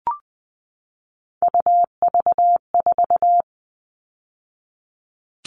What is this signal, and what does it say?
Morse code, heard as a keyed tone, about 20 wpm, 700 Hz. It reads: UV4
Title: All Cut Numbers - 3-Digit 20wpm